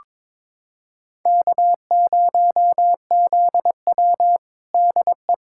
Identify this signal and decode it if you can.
Morse code, heard as a keyed tone, about 22 wpm, 700 Hz. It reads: K0ZW DE
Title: USA Callsign DE Callsign - 4 Characters - 22wpm